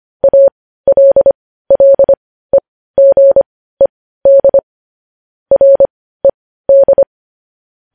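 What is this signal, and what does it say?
Morse code, heard as a keyed tone, about 25 wpm, spaced as for 15 wpm, 550 Hz. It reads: ALLEGED RED
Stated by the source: Morse news headlines sent at 15 wpm.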